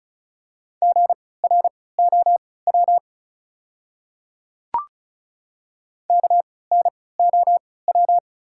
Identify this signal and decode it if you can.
Morse code, heard as a keyed tone, about 36 wpm, spaced as for 20 wpm, 700 Hz. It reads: GROW KNOW
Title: 3x Character Spacing - Top 300 Words - 35wpm